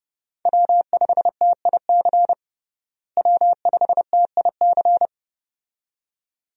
Morse code, heard as a keyed tone, about 32 wpm, 700 Hz. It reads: W5TIC W5TIC
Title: USA Call Signs - 5 Characters - Repeated 2x - 30wpm